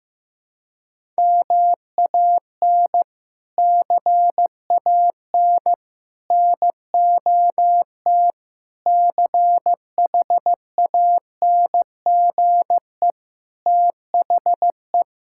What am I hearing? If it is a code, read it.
Morse code, heard as a keyed tone, about 14 wpm, 700 Hz. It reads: MAN CAN NOT CHANGE THE
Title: Sentences from Top 1000 Words - Review Part 5 - 15wpm